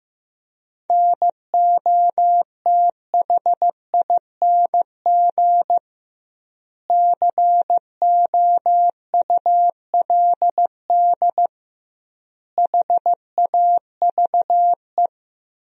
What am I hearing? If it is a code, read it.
Morse code, heard as a keyed tone, about 15 wpm, 700 Hz. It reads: NOTHING COULD HAVE
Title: 2x Word-Spacing - Sentences from Top 500 Words - 15wpm